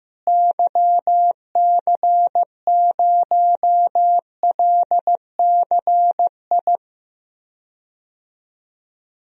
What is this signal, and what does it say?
Morse code, heard as a keyed tone, about 16 wpm, 700 Hz. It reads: YC0LCI
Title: DXCC-100 Call Signs - 6 Characters - Repeated 3x - 15wpm